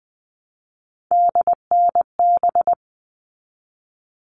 Morse code, heard as a keyed tone, about 20 wpm, 700 Hz. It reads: DNB
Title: All Cut Numbers - 3-Digit 20wpm